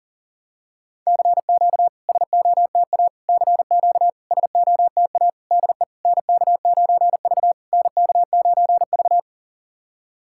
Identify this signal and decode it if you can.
Morse code, heard as a keyed tone, about 40 wpm, 700 Hz. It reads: CQ SOTA CQ SOTA DE NK9V NK9V
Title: SOTA QSO Practice 40wpm